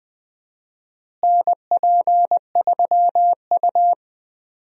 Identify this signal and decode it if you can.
Morse code, heard as a keyed tone, about 20 wpm, 700 Hz. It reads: NP3U
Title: USA Call Signs - 4 Characters - Repeated 2x - 20wpm